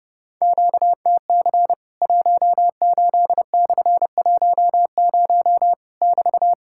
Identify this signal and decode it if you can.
Morse code, heard as a keyed tone, about 30 wpm, 700 Hz. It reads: QTC 18/10 <BT>
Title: Worked All Europe DX Contest - QTC Exchange - 30wpm